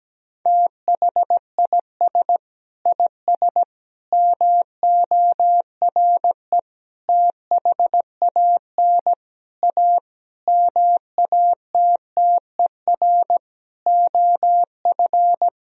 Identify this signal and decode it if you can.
Morse code, heard as a keyed tone, about 17 wpm, 700 Hz. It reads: THIS IS MORE THAN A MATTER OF
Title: Sentences from Top 1000 Words - Review Part 3 - 17wpm